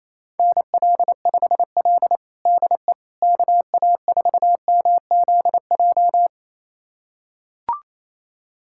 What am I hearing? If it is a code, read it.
Morse code, heard as a keyed tone, about 28 wpm, 700 Hz. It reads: NL5L DE KA4MZJ
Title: USA Callsign DE Callsign 28wpm